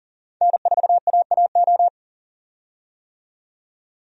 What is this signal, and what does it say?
Morse code, heard as a keyed tone, about 40 wpm, 700 Hz. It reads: N4AAO
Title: USA Call Signs - 5 Characters - 40wpm